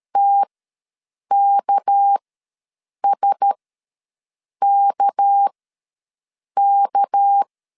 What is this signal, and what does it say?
Morse code, heard as a keyed tone, about 13 wpm, spaced as for 7 wpm, 800 Hz. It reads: TKSKK
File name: LEZIONI / 6 PKX0 / 6 PK ws 12wpm 3116